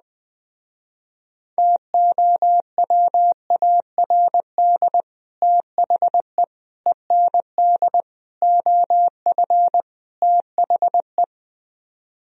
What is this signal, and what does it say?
Morse code, heard as a keyed tone, about 20 wpm, 700 Hz. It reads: TOWARD THE END OF THE